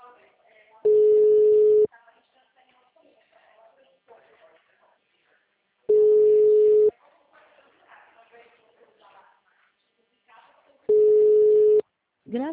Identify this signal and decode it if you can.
Ringback tone (Europe, 425 Hz), with speech.